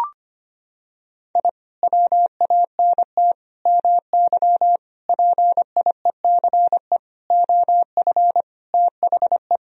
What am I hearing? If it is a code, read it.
Morse code, heard as a keyed tone, about 25 wpm, 700 Hz. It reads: I WANT MY PIECE OF THE
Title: Sentences from Top 1000 Words - Review Part 2 - 25wpm